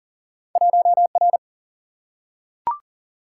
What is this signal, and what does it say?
Morse code, heard as a keyed tone, about 40 wpm, 700 Hz. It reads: 1R
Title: Single-Letter Single-Number 40wpm